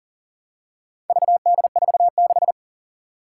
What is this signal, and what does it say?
Morse code, heard as a keyed tone, about 40 wpm, 700 Hz. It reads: VD46
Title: Four Challenging Letter-Number Combinations 40wpm